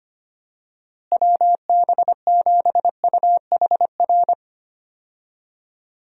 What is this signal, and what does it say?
Morse code, heard as a keyed tone, about 25 wpm, 700 Hz. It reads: WB7UHR
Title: USA Call Signs - 6 Characters - Repeated 3x - 25wpm